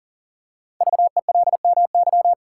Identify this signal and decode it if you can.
Morse code, heard as a keyed tone, about 39 wpm, 700 Hz. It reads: VELMY
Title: Code Groups - 5 Letters - 40wpm